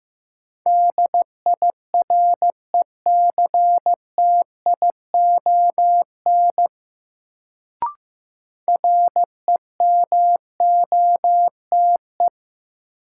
Morse code, heard as a keyed tone, about 15 wpm, 700 Hz. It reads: DIRECTION REMOTE
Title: Sentences from Top 1300 Words - New Word Drill Alt - 15wpm